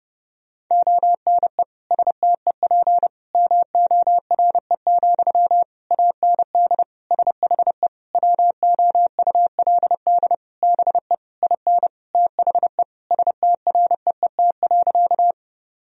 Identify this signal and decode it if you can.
Morse code, heard as a keyed tone, about 30 wpm, 700 Hz. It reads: ONE STEP MORE, AND SHE WOULD BE IN THE STREET.
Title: Sentences from Top 1000 Words - Review Part 3 - 30wpm